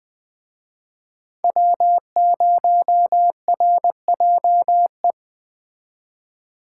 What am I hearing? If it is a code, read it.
Morse code, heard as a keyed tone, about 20 wpm, 700 Hz. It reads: W0RJE